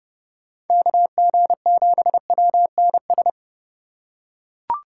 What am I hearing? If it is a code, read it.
Morse code, heard as a keyed tone, about 31 wpm, 700 Hz. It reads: KG7WNS